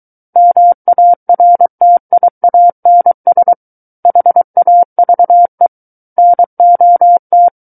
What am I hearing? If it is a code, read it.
Morse code, heard as a keyed tone, about 23 wpm, 700 Hz. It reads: MARTIANS HAVE NOT
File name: Book One / 14-Chapter 14 - 23 WPM